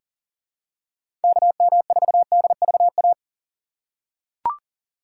Morse code, heard as a keyed tone, about 38 wpm, 700 Hz. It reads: KM4DVA
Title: USA Call Signs - 6 Characters - 2x Character Spacing - 40wpm